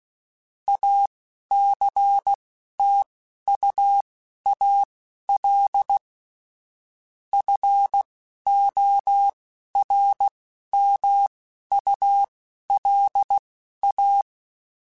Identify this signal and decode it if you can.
Morse code, heard as a keyed tone, about 16 wpm, spaced as for 11 wpm, 800 Hz. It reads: ACTUAL FORMULA
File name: Technical Subjects / Technical Subjects-mp3s / XC @10